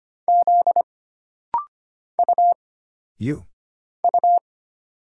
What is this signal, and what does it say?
Morse code, heard as a keyed tone, about 25 wpm, 700 Hz. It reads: Z U U
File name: Single Character (Letter:Number:Punctuation:Prosign) - Rapid-Fire - 25wpm